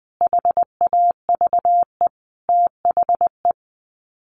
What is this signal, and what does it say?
Morse code, heard as a keyed tone, about 20 wpm, 700 Hz. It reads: HAVE THE
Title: Sets of 5 Words 20wpm